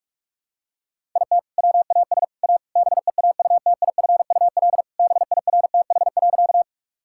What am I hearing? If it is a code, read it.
Morse code, heard as a keyed tone, about 46 wpm, 700 Hz. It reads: IT WAS A BEAUTIFUL BIRTH.
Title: Sentences from Top 1000 Words - Review Part 5 - 45wpm